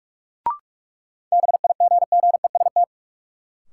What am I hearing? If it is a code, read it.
Morse code, heard as a keyed tone, about 46 wpm, 700 Hz. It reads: BIGGEST
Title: Sets of 4 Words - New Word Drill - 45wpm